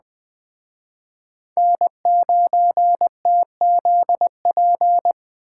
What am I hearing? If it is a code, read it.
Morse code, heard as a keyed tone, about 20 wpm, 700 Hz. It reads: N9TZP
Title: Speed Racing - USA 1x3 Call Signs 17wpm